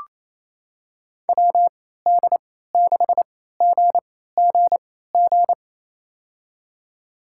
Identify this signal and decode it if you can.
Morse code, heard as a keyed tone, about 28 wpm, spaced as for 16 wpm, 700 Hz. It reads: WD6GGG
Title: USA Call Signs - 6 Characters - 3x Character Spacing - 28wpm